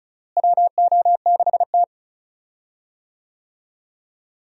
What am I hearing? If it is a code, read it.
Morse code, heard as a keyed tone, about 34 wpm, 700 Hz. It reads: WO6T
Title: USA Call Signs - 4 Characters - Repeated 3x - 35wpm